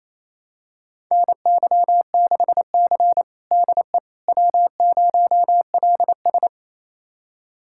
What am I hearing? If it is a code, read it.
Morse code, heard as a keyed tone, about 28 wpm, 700 Hz. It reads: NY6C DE W0LS
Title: USA Callsign DE Callsign - 4 Characters - 28wpm